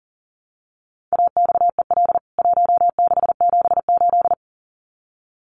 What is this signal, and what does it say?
Morse code, heard as a keyed tone, about 40 wpm, 700 Hz. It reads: AXEL 1678
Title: CWT Exchange 40wpm